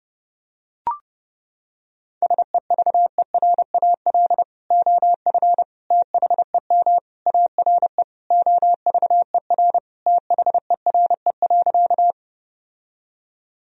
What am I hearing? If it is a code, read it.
Morse code, heard as a keyed tone, about 32 wpm, 700 Hz. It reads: SEVERAL OF THEM ARE OVER THERE.
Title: Sentences from Top 1000 Words - Review Part 2 - 30wpm